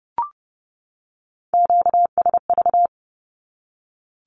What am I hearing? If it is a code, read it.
Morse code, heard as a keyed tone, about 30 wpm, 700 Hz. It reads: QSV
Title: Q-Codes 30wpm